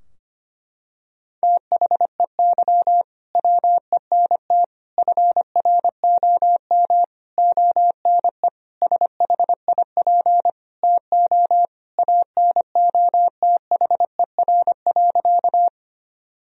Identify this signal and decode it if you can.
Morse code, heard as a keyed tone, about 25 wpm, 700 Hz. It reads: THEY WENT FROM ONE SHIP TO ANOTHER.
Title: Sentences from Top 1000 Words - Review Part 2 - 25wpm